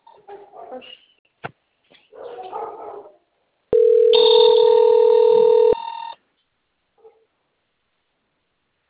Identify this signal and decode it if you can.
Ringback tone (North America), 440 + 480 Hz.